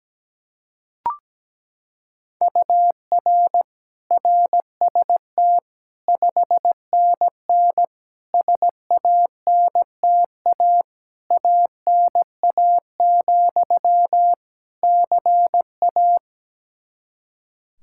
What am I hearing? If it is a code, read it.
Morse code, heard as a keyed tone, about 17 wpm, 700 Hz. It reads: UR RST 5NN SANTA ANA, CA
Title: Practice QSO - Exchange 1 - Beginning - 17wpm